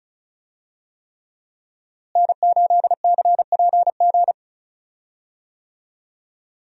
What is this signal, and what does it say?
Morse code, heard as a keyed tone, about 34 wpm, 700 Hz. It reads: N8CPG
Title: USA Call Signs - 5 Characters - Repeated 3x - 35wpm